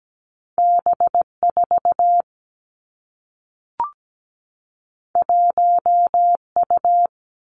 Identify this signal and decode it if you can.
Morse code, heard as a keyed tone, about 17 wpm, 700 Hz. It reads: B 4 1 U
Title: Two Challenging Letter-Number Combinations 17wpm